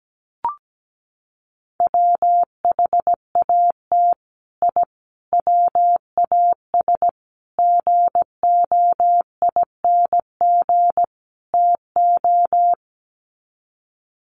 Morse code, heard as a keyed tone, about 17 wpm, 700 Hz. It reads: WHAT I WAS GOING TO
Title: Sets of 5 Words Encore2 17wpm